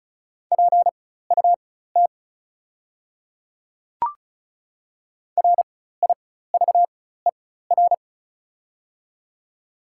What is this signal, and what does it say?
Morse code, heard as a keyed tone, about 36 wpm, spaced as for 16 wpm, 690 Hz. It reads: PUT RIVER
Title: 4x Character Spacing - Top 300 Words - 35wpm